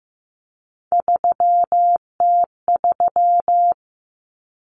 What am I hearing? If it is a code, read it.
Morse code, heard as a keyed tone, about 15 wpm, 700 Hz. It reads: 3T3
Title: Common Cut Numbers - 3-Digit 15wpm